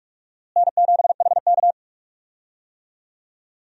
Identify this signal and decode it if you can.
Morse code, heard as a keyed tone, about 46 wpm, 700 Hz. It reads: N7HK
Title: USA Call Signs - 4 Characters - Repeated 2x - 45wpm